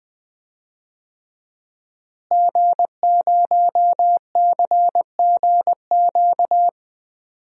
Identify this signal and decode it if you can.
Morse code, heard as a keyed tone, about 20 wpm, 700 Hz. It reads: G0CGQ
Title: DXCC-100 Call Signs - 5 Characters - Repeated 3x - 20wpm